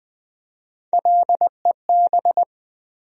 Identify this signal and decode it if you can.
Morse code, heard as a keyed tone, about 20 wpm, 700 Hz. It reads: LEB
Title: USA Call Sign Postfixes - 3 Letters 20wpm